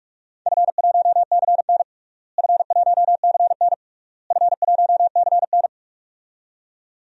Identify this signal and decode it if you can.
Morse code, heard as a keyed tone, about 42 wpm, 700 Hz. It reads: F1CN F1CN F1CN
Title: DXCC-100 Call Signs - 4 Characters - Repeated 3x - 45wpm